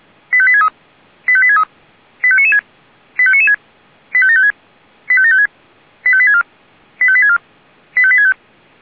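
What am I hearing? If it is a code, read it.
Five-tone selective call (ZVEI), heard as quick runs of five tones, tones of about 70 ms. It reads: 85863 85863 85996 85996 85756 85756 85864 85864 85865
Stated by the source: ZVEI five-tone selective call received on SDR.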